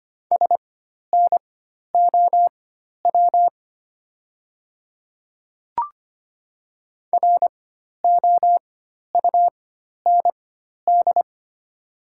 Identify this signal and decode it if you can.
Morse code, heard as a keyed tone, about 25 wpm, spaced as for 12 wpm, 700 Hz. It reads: SNOW ROUND
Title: 4x Character Spacing - Top 500 Words - 25wpm